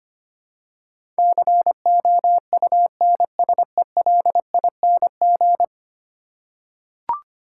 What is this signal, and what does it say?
Morse code, heard as a keyed tone, about 25 wpm, 700 Hz. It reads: COUNSELING
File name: Ten Letter Words - 25wpm